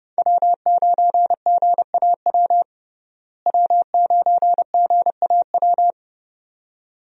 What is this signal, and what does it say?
Morse code, heard as a keyed tone, about 32 wpm, 700 Hz. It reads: W9GAW W9GAW
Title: USA Call Signs - 5 Characters - Repeated 2x - 30wpm